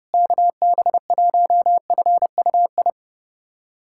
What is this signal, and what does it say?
Morse code, heard as a keyed tone, about 32 wpm, 700 Hz. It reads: KB1FUI
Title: USA Callsign DE Callsign - 6 Characters - 30wpm